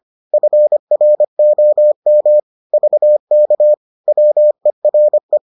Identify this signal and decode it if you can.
Morse code, heard as a keyed tone, about 25 wpm, 600 Hz. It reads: FROM VK WERE